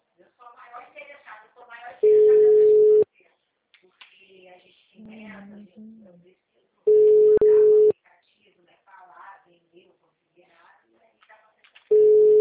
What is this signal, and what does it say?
Ringback tone (Europe, 425 Hz), with speech.
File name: out- -2014- - - .5510